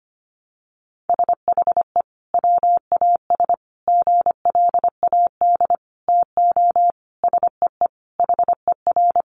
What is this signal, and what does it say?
Morse code, heard as a keyed tone, about 25 wpm, 700 Hz. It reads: SHE WAS GLAD TO SEE HER
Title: Sentences from Top 800 Words 25wpm